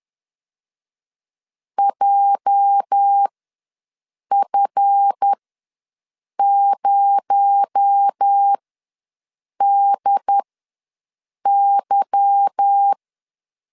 Morse code, heard as a keyed tone, about 10 wpm, spaced as for 6 wpm, 800 Hz. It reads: JF0DY
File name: LEZIONI / 8 YJ45 / 8 YJ ws 10wpm 4043